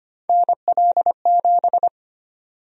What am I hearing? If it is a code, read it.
Morse code, heard as a keyed tone, about 25 wpm, 700 Hz. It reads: NL7
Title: USA Call Sign Prefixes - Repeated 2x - 25wpm